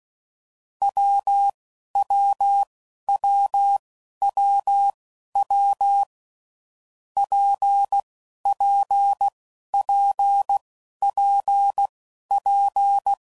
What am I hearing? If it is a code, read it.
Morse code, heard as a keyed tone, about 16 wpm, spaced as for 11 wpm, 800 Hz. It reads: WWWWW PPPPP
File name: Combinations / Difficult Combos / WPJ WPJ1